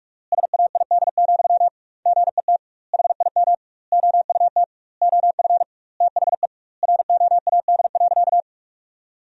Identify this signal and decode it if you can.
Morse code, heard as a keyed tone, about 44 wpm, 700 Hz. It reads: SAID, GET HIM OUT OF THE ROAD.
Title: 2x Word-Spacing - Sentences from Top 400 Words - 45wpm